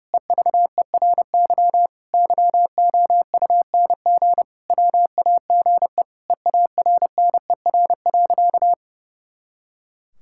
Morse code, heard as a keyed tone, about 30 wpm, 700 Hz. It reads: EVERY YOUNG WAGE EARNER.
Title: Pangram Challenge v2 - 30wpm